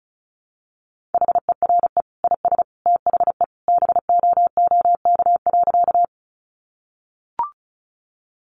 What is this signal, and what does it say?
Morse code, heard as a keyed tone, about 34 wpm, 700 Hz. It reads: HERE IS THE BOOK.
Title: Sentences from Top 300 Words 35wpm